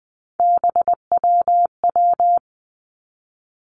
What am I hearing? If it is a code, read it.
Morse code, heard as a keyed tone, about 20 wpm, 700 Hz. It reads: BWW
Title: Three Challenging Letter-Number Combinations 20wpm